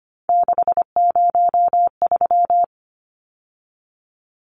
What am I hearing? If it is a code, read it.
Morse code, heard as a keyed tone, about 25 wpm, 700 Hz. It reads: 603
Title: Numbers - 3-Digit 25wpm